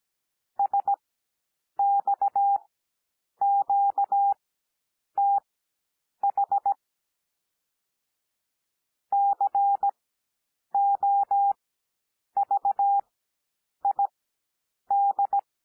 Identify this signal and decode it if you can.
Morse code, heard as a keyed tone, about 17 wpm, spaced as for 8 wpm, 800 Hz. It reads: SXQTH COVID
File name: cwmp3 / L2 SOUND LETTERS 17 wpm